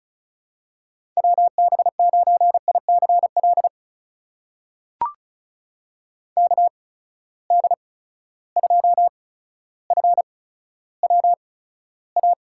Morse code, heard as a keyed tone, about 36 wpm, spaced as for 15 wpm, 700 Hz. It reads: WB9ICL KD2FWA
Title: USA Call Signs - 6 Characters - 8x Character Spacing - 35wpm